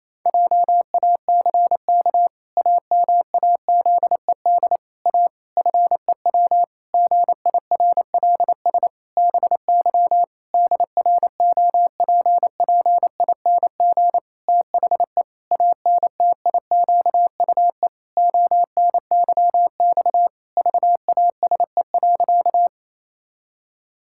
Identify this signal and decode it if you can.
Morse code, heard as a keyed tone, about 28 wpm, 700 Hz. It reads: JACK AMAZED A FEW GIRLS BY DROPPING THE ANTIQUE ONYX VASE.
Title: Pangram Challenge v2 - 28wpm